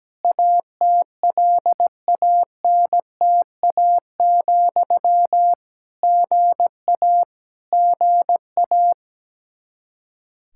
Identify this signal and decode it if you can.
Morse code, heard as a keyed tone, about 17 wpm, 700 Hz. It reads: ATLANTA, GA GA
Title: Practice QSO - Exchange 1 - Beginning - 17wpm